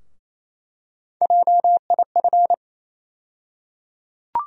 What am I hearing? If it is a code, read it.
Morse code, heard as a keyed tone, about 28 wpm, 700 Hz. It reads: JIF
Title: USA Call Sign Postfixes - 3 Letters 28wpm